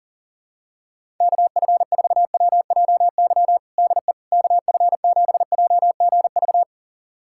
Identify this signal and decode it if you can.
Morse code, heard as a keyed tone, about 37 wpm, 700 Hz. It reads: KF4WJY DE KF7JGV
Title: USA Callsign DE Callsign 40wpm